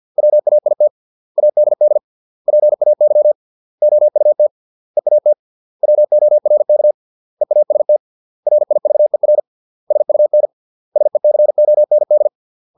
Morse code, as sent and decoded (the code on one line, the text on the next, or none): .-- .- .. - / .- -.. -.. / .--. .- -.-- / --- ..- - / . .- - / .-- --- .-. -.- / . .- ... - / .-. .. ...- . .-. / ... ..- -. / ... -.-. --- -. -..
WAIT ADD PAY OUT EAT WORK EAST RIVER SUN SECOND